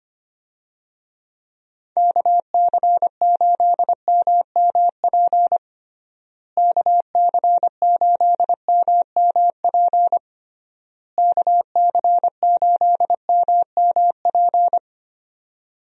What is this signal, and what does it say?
Morse code, heard as a keyed tone, about 25 wpm, 700 Hz. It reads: KC8MMP KC8MMP KC8MMP
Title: USA Call Signs - 6 Characters - Repeated 3x - 25wpm